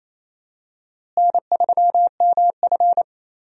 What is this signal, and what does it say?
Morse code, heard as a keyed tone, about 28 wpm, 700 Hz. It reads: N3MF
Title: USA Call Signs - 4 Characters - Repeated 3x - 28wpm